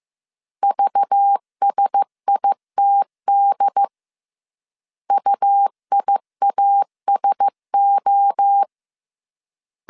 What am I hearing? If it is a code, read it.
Morse code, heard as a keyed tone, about 15 wpm, 800 Hz. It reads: VSITD UIASO
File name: CW / DET / 1 6 1500C S C 14wpm 2951